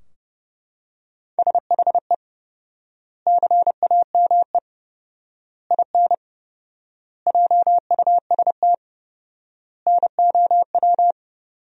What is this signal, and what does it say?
Morse code, heard as a keyed tone, about 30 wpm, 700 Hz. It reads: SHE CAME IN JUST NOW
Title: 4x Word-Spacing - Sentences from Top 300 Words - 30wpm